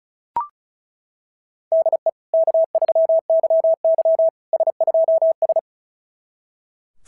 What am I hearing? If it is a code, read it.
Morse code, heard as a keyed tone, about 36 wpm, 690 Hz. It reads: DE K3YY S2S
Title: SOTA QSO Practice 35wpm